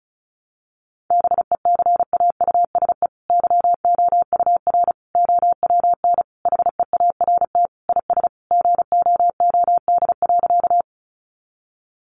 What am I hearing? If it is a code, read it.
Morse code, heard as a keyed tone, about 34 wpm, 700 Hz. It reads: BECAUSE YOUR OWN HEART IS GOOD.
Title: Sentences from Top 1200 Words 35wpm